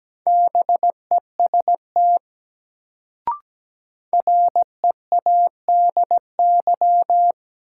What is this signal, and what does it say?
Morse code, heard as a keyed tone, about 17 wpm, 700 Hz. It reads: BEST READY
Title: Sentences from Top 400 Words - New Word Drill - 17wpm